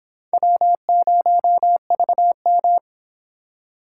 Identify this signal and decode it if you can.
Morse code, heard as a keyed tone, about 25 wpm, 700 Hz. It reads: W0VM